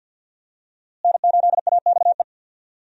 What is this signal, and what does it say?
Morse code, heard as a keyed tone, about 49 wpm, 700 Hz. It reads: N8AXE
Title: USA Call Signs - 5 Characters - Repeated 3x - 50wpm